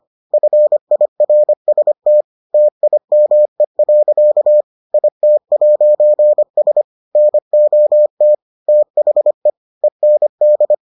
Morse code, heard as a keyed tone, about 25 wpm, 600 Hz. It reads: FIRST TIME. IT'S NOT THE END